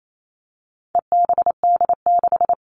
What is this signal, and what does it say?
Morse code, heard as a keyed tone, about 28 wpm, 700 Hz. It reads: EBD6
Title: All Cut Numbers - 4-Digit 28wpm